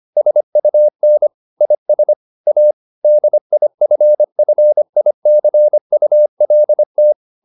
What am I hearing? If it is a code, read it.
Morse code, heard as a keyed tone, about 25 wpm, 600 Hz. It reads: SUN IS A DIFFICULT